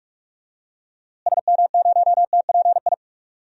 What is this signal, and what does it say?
Morse code, heard as a keyed tone, about 43 wpm, 700 Hz. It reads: SM0TPI
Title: DXCC-100 Call Signs - 6 Characters - Repeated 2x - 45wpm